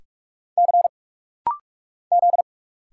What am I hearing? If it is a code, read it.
Morse code, heard as a keyed tone, about 46 wpm, 700 Hz. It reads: C Z
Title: Single Letters - Rapid-Fire 45wpm